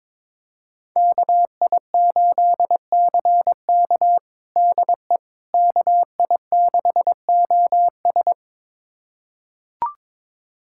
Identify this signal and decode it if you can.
Morse code, heard as a keyed tone, about 21 wpm, 700 Hz. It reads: KI8CK DE KI6OS
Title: USA Callsign DE Callsign - 5 Characters - 22wpm